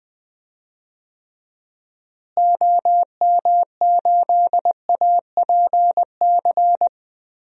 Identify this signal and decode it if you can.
Morse code, heard as a keyed tone, about 21 wpm, 700 Hz. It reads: OM8APC
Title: DXCC-100 Call Signs - 6 Characters - Repeated 3x - 20wpm